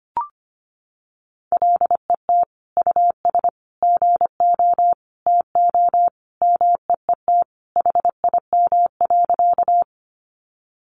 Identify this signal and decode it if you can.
Morse code, heard as a keyed tone, about 25 wpm, 700 Hz. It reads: LET US GO TO MEET HIM.
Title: Sentences from Top 1100 Words 25wpm